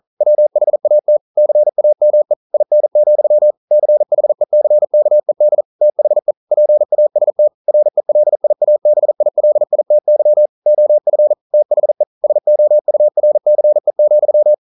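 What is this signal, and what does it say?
Morse code, heard as a keyed tone, about 43 wpm, 600 Hz. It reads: WHAT CAME IN, CHECKED THE PAST RELIABILITY OF THE SOURCE,
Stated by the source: Morse code practice at 41 wpm, from CW med Gnister.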